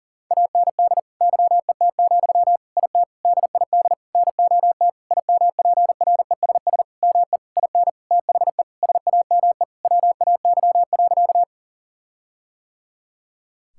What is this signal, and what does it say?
Morse code, heard as a keyed tone, about 37 wpm, 700 Hz. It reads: AND YET, IT DID NOT IMPRESS ME IN THE SAME WAY.
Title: Sentences from Top 1600 Words 40wpm